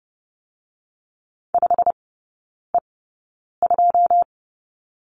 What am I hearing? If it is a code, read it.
Morse code, heard as a keyed tone, about 32 wpm, spaced as for 9 wpm, 700 Hz. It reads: 5E2
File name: Lesson 18e - Letter m - Three Characters 30wpm